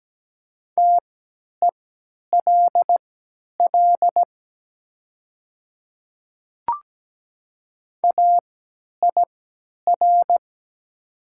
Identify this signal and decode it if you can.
Morse code, heard as a keyed tone, about 17 wpm, spaced as for 10 wpm, 700 Hz. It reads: TELL AIR
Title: 3x Character Spacing - Top 200 Words - 17wpm